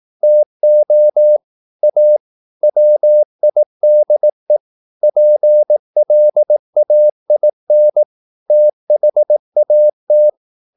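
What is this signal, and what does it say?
Morse code, heard as a keyed tone, about 18 wpm, 600 Hz. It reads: TO A WIDE PLAIN THAT
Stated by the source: Morse practice at 18 wpm, English text, from CW med Gnister.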